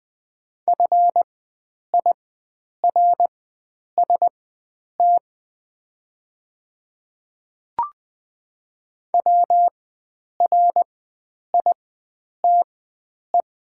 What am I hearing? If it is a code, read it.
Morse code, heard as a keyed tone, about 20 wpm, spaced as for 9 wpm, 700 Hz. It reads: FIRST WRITE
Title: 4x Character Spacing - Top 100 Words - 20wpm